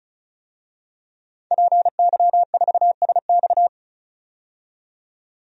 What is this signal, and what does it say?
Morse code, heard as a keyed tone, about 36 wpm, 700 Hz. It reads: PY4SX
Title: DXCC-100 Call Signs - 5 Characters - 35wpm